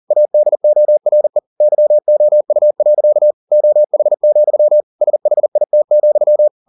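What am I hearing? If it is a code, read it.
Morse code, heard as a keyed tone, about 40 wpm, 600 Hz. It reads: ADORE YOU. OH, SHIT,
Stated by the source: Morse code at 40 wpm, practice text, CW med Gnister.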